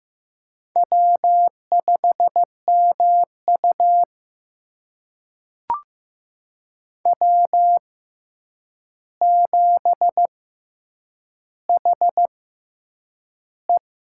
Morse code, heard as a keyed tone, about 15 wpm, spaced as for 8 wpm, 700 Hz. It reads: W5MU W7HE
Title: USA Call Signs - 4 Characters - 6x Character Spacing - 15wpm